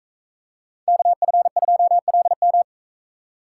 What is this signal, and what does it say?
Morse code, heard as a keyed tone, about 41 wpm, 700 Hz. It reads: KF2LM